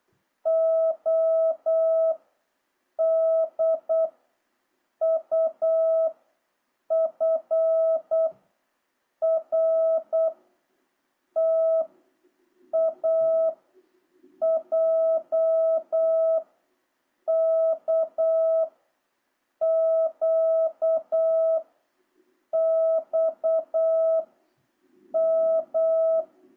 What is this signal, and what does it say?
Morse code, heard as a keyed tone, about 8 wpm, 650 Hz. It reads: O D U F R T A J K Q X M
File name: jh8cbh / 2026cw